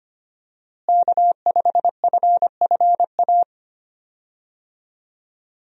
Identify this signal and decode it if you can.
Morse code, heard as a keyed tone, about 25 wpm, 700 Hz. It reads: K5FFA
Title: USA Call Signs - 5 Characters - Repeated 3x - 25wpm